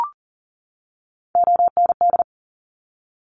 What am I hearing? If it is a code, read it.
Morse code, heard as a keyed tone, about 40 wpm, 700 Hz. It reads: OND
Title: Top Three Letter Combinations in Words 40wpm